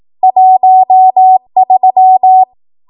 Morse code, heard as a keyed tone, about 18 wpm, 750 Hz. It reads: 13
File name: Morse / 18WPM